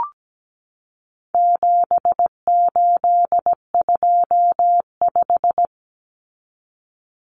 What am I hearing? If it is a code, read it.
Morse code, heard as a keyed tone, about 17 wpm, 700 Hz. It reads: 7825
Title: Numbers - 4-Digit 17wpm